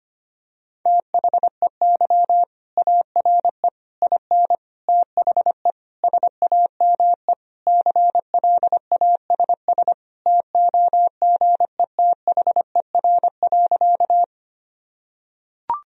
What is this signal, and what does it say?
Morse code, heard as a keyed tone, about 25 wpm, 700 Hz. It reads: THEY ARE IN THE SAME CLASS TOGETHER.
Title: Sentences from Top 1000 Words - Review Part 2 - 25wpm